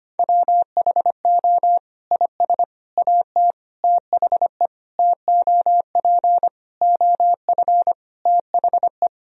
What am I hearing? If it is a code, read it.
Morse code, heard as a keyed tone, about 25 wpm, 700 Hz. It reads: WHO IS AT THE TOP OF THE
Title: Sentences from Top 1000 Words - Review Part 4 - 25wpm